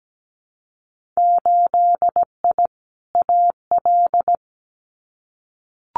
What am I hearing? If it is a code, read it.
Morse code, heard as a keyed tone, about 17 wpm, 700 Hz. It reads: 8I AL
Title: Winter Field Day Exchanges Encore 17wpm